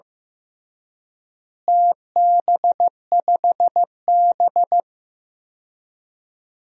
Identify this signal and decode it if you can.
Morse code, heard as a keyed tone, about 16 wpm, 700 Hz. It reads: TB5B
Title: DXCC-100 Call Signs - 4 Characters - Repeated 3x - 15wpm